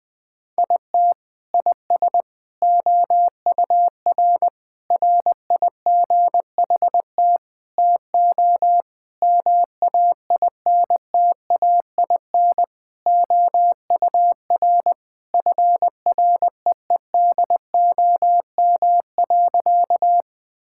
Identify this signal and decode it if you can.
Morse code, heard as a keyed tone, about 20 wpm, 700 Hz. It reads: IT IS OUR RIGHT TO MAINTAIN OUR FREEDOM.
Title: Sentences from Top 1000 Words - Review Part 5 - 20wpm